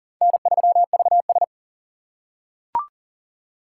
Morse code, heard as a keyed tone, about 40 wpm, 700 Hz. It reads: N3VS
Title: USA Call Signs - 4 Characters - 6x Character Spacing - 40wpm